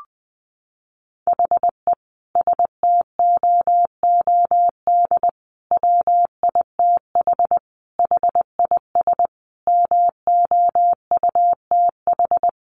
Morse code, heard as a keyed tone, about 20 wpm, 700 Hz. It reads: HE STOOD WITH HIS MOUTH
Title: Sentences from Top 1500 Words 20wpm